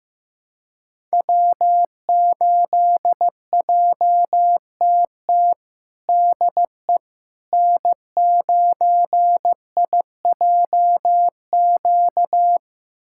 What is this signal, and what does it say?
Morse code, heard as a keyed tone, about 15 wpm, 700 Hz. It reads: W8JTT DE N9IJQ
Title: USA Callsign DE Callsign - 5 Characters - 15wpm